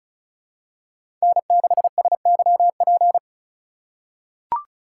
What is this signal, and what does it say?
Morse code, heard as a keyed tone, about 36 wpm, 700 Hz. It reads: N6SYP
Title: USA Call Signs - 3x Character Spacing - 35wpm